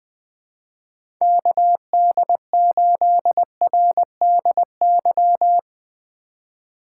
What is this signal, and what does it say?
Morse code, heard as a keyed tone, about 20 wpm, 700 Hz. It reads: KD8RDY
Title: USA Call Signs - 6 Characters - 3x Character Spacing - 20wpm